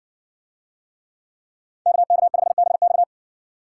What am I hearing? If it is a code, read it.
Morse code, heard as a keyed tone, about 60 wpm, 700 Hz. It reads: KK5BX